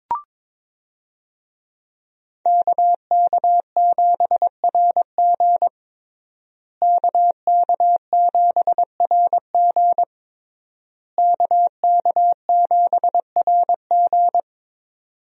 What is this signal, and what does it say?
Morse code, heard as a keyed tone, about 22 wpm, 700 Hz. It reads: KK7RG KK7RG KK7RG
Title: USA Call Signs - 5 Characters - Repeated 3x - 22wpm